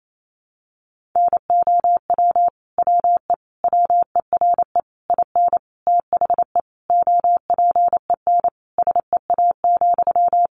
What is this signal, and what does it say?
Morse code, heard as a keyed tone, about 28 wpm, 700 Hz. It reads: NOW WE WERE IN THE OPEN SEA,
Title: Sentences from Top 1200 Words 28wpm